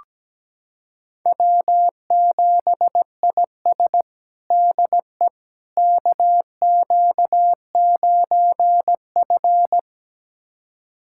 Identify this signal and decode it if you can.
Morse code, heard as a keyed tone, about 17 wpm, 700 Hz. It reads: W7IS DE KQ9F
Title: USA Callsign DE Callsign - 4 Characters - 17wpm